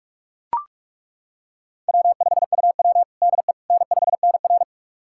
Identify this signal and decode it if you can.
Morse code, heard as a keyed tone, about 42 wpm, 700 Hz. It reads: W5UW DE N5NR
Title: USA Callsign DE Callsign - 4 Characters - 45wpm